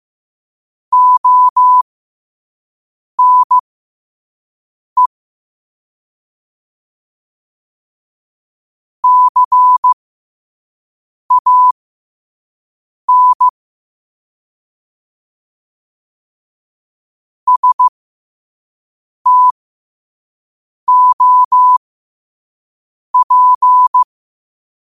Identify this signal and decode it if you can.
Morse code, heard as a keyed tone, about 15 wpm, spaced as for 5 wpm, 1000 Hz. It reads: ONE CAN STOP
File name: qotdfiles / 5wpm